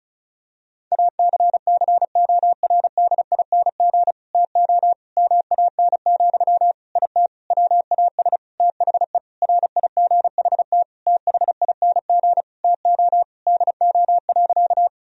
Morse code, heard as a keyed tone, about 34 wpm, 700 Hz. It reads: ACCORDING TO MAN, IT WAS THE RIGHT THING TO DO.
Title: Sentences from Top 1000 Words - Review Part 4 - 35wpm